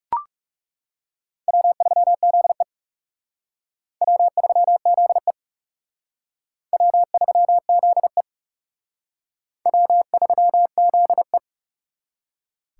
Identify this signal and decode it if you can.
Morse code, heard as a keyed tone, about 37 wpm, 700 Hz. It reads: W3ZE W3ZE W3ZE W3ZE
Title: Speed Racing - USA 1x2 Call Signs 30wpm